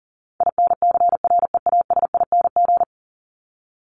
Morse code, heard as a keyed tone, about 40 wpm, 700 Hz. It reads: INCREASING
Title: ING Suffix - Top 100 Words with Base Words 40wpm